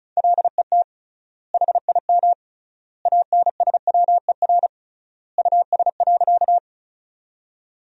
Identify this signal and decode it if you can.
Morse code, heard as a keyed tone, about 34 wpm, 700 Hz. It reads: LET HIM ANSWER US.
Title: 3x Word-Spacing - Sentences from Top 300 Words - 35wpm